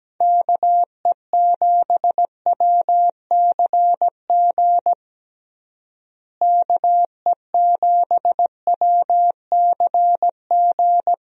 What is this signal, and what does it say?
Morse code, heard as a keyed tone, about 17 wpm, 700 Hz. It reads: KE7WCG KE7WCG
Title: USA Call Signs - 6 Characters - Repeated 3x - 17wpm